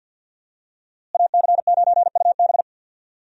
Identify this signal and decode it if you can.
Morse code, heard as a keyed tone, about 46 wpm, 700 Hz. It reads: AC9UB
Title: USA Call Signs - 5 Characters - Repeated 2x - 50wpm